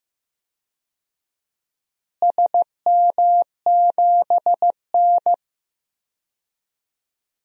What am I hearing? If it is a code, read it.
Morse code, heard as a keyed tone, about 15 wpm, 700 Hz. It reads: SM7N
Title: DXCC-100 Call Signs - 4 Characters - Repeated 2x - 15wpm